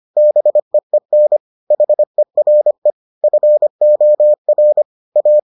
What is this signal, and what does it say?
Morse code, heard as a keyed tone, about 25 wpm, 600 Hz. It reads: BEEN HERE FOR A